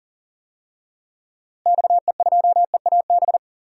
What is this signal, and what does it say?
Morse code, heard as a keyed tone, about 40 wpm, 700 Hz. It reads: XE2EAB
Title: DXCC-100 Call Signs - 6 Characters - Repeated 3x - 40wpm